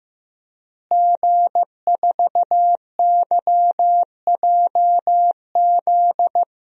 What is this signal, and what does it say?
Morse code, heard as a keyed tone, about 16 wpm, 700 Hz. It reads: G4YJZ
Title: DXCC-100 Call Signs - 5 Characters - Repeated 3x - 15wpm